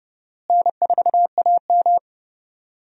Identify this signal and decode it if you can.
Morse code, heard as a keyed tone, about 30 wpm, 700 Hz. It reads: N4AM
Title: USA Call Signs - 4 Characters - Repeated 3x - 30wpm